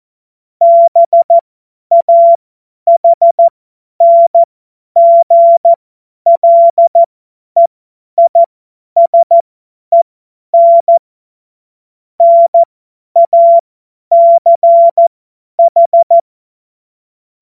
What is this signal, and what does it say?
Morse code, heard as a keyed tone, about 14 wpm, spaced as for 10 wpm, 680 Hz. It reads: BAHNGLEISEN NACH
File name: Morsekurs / Witze / Witz 13. Juli 2021, 50 F70